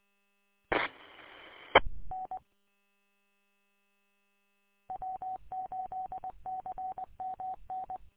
Morse code, heard as a keyed tone, about 20 wpm, spaced as for 16 wpm, 750 Hz. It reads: TEN W8CMN
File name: w8cmn- -combined